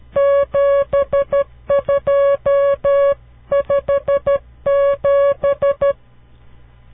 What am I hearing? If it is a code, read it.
Morse code, heard as a keyed tone, about 12 wpm, 560 Hz. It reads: 7257
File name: CW Learning Programs / farnsworth / unit 5